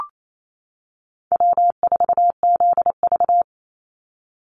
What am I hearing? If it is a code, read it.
Morse code, heard as a keyed tone, about 28 wpm, 700 Hz. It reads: W4ZV
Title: NA SOTA Chaser Callsigns 28wpm